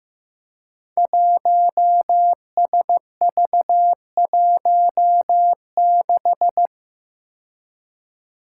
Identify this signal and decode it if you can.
Morse code, heard as a keyed tone, about 15 wpm, 700 Hz. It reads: 1SV16
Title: Five Challenging Letter-Number Combinations 15wpm